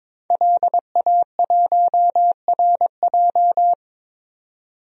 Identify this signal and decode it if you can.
Morse code, heard as a keyed tone, about 22 wpm, 700 Hz. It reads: LA1RJ
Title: DXCC-100 Call Signs - 5 Characters - 22wpm